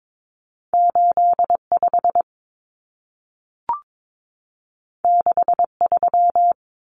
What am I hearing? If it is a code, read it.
Morse code, heard as a keyed tone, about 22 wpm, 700 Hz. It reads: 85 63
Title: Numbers - 2-Digit 22wpm